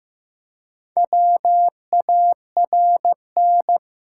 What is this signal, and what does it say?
Morse code, heard as a keyed tone, about 15 wpm, 700 Hz. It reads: WARN
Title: Four Letter Words - 15wpm